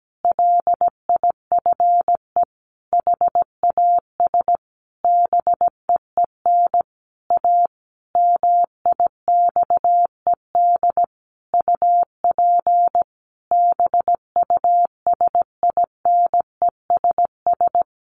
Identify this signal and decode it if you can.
Morse code, heard as a keyed tone, about 17 wpm, 700 Hz. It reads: LIFE HAS BEEN A MIXED UP BUSINESS
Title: Sentences from Top 1500 Words 17wpm